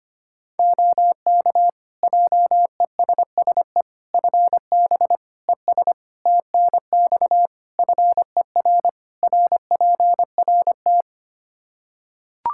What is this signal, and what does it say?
Morse code, heard as a keyed tone, about 25 wpm, 700 Hz. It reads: OK JESSE FB ES TNX FER RPRT
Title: Practice QSO - Exchange 2 - Info - 25wpm